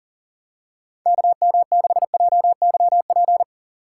Morse code, heard as a keyed tone, about 40 wpm, 700 Hz. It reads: KM6JYP
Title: USA Call Signs - 6 Characters - 2x Character Spacing - 40wpm